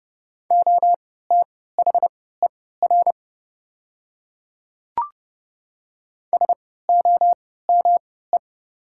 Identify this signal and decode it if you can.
Morse code, heard as a keyed tone, about 30 wpm, spaced as for 17 wpm, 700 Hz. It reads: OTHER SOME
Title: 3x Character Spacing - Top 100 Words - 30wpm